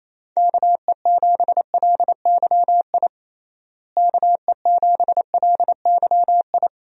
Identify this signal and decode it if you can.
Morse code, heard as a keyed tone, about 28 wpm, 700 Hz. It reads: KE7LYI KE7LYI
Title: USA Call Signs - 6 Characters - Repeated 2x - 28wpm